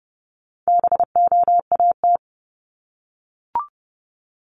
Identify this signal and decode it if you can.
Morse code, heard as a keyed tone, about 30 wpm, 700 Hz. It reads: BOAT
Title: Multi-letter Phonemes Repeated 3x 30wpm